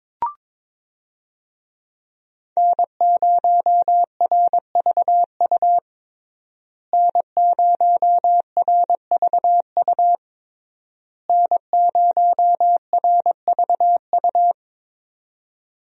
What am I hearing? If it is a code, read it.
Morse code, heard as a keyed tone, about 22 wpm, 700 Hz. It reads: N0RVU N0RVU N0RVU
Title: USA Call Signs - 5 Characters - Repeated 3x - 22wpm